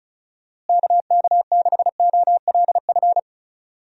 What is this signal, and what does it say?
Morse code, heard as a keyed tone, about 34 wpm, 700 Hz. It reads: KK6OLF
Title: USA Call Signs - 6 Characters - Repeated 3x - 35wpm